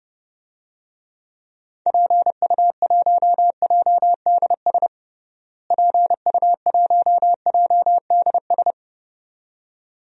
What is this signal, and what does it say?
Morse code, heard as a keyed tone, about 31 wpm, 700 Hz. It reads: PU1JDS PU1JDS
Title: DXCC-100 Call Signs - 6 Characters - Repeated 2x - 30wpm